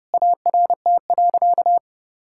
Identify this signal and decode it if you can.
Morse code, heard as a keyed tone, about 30 wpm, 700 Hz. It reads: ART.
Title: Sentences from Top 1000 Words - Review Part 6 - 30wpm